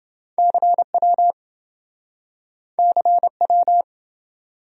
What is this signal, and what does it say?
Morse code, heard as a keyed tone, about 27 wpm, 700 Hz. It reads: CW CW